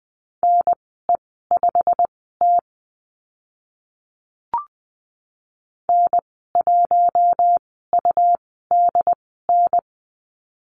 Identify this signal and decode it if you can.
Morse code, heard as a keyed tone, about 20 wpm, spaced as for 14 wpm, 700 Hz. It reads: NE5T N1UDN
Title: 161 Lesson 26g - Letter g - Call Signs 20wpm